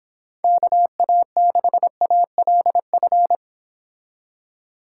Morse code, heard as a keyed tone, about 25 wpm, 700 Hz. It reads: KA6ALF